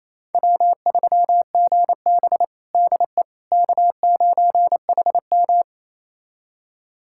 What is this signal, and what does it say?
Morse code, heard as a keyed tone, about 28 wpm, 700 Hz. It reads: W3GB DE K9HM
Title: USA Callsign DE Callsign - 4 Characters - 28wpm